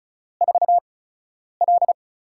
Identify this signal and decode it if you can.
Morse code, heard as a keyed tone, about 36 wpm, spaced as for 10 wpm, 700 Hz. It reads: 4L
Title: USA Call Signs - 4 Characters - 8x Character Spacing - 35wpm